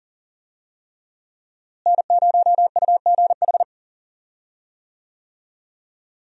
Morse code, heard as a keyed tone, about 40 wpm, 700 Hz. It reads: N0UGH
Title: USA Call Signs - 5 Characters - Repeated 3x - 40wpm